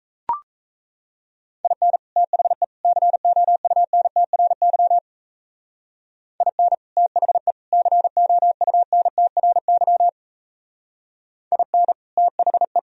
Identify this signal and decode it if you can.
Morse code, heard as a keyed tone, about 35 wpm, 700 Hz. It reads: IN THE COUNTRY IN THE COUNTRY IN THE